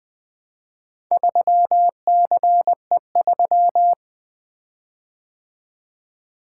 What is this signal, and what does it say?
Morse code, heard as a keyed tone, about 20 wpm, 700 Hz. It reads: 3CE3
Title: Four Letter-Number Combinations - 20wpm